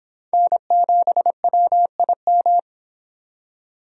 Morse code, heard as a keyed tone, about 26 wpm, 700 Hz. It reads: N7WIM